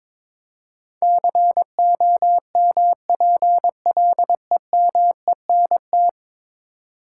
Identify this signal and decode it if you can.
Morse code, heard as a keyed tone, about 22 wpm, 700 Hz. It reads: COMPLEMENT
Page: morse-code-ninja-practice-22wpm